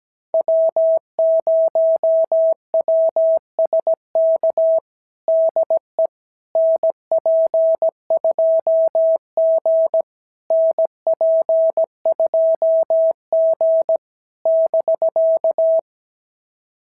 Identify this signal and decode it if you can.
Morse code, heard as a keyed tone, about 17 wpm, 660 Hz. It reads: W0WSK DE NP2G NP2G <BK>
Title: SOTA QSO Practice 17wpm